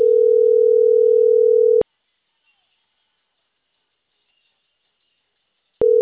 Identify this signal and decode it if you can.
Ringback tone (North America), 440 + 480 Hz.